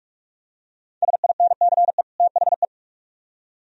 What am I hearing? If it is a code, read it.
Morse code, heard as a keyed tone, about 44 wpm, 700 Hz. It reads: SINCE THE